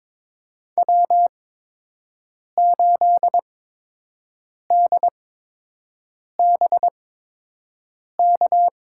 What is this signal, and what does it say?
Morse code, heard as a keyed tone, about 22 wpm, spaced as for 6 wpm, 700 Hz. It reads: W8DBK
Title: USA Call Signs - 5 Characters - 8x Character Spacing - 22wpm